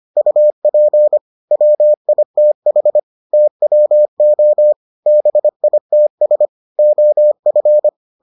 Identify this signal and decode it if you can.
Morse code, heard as a keyed tone, about 25 wpm, 600 Hz. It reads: UP WITH TWO BITS OF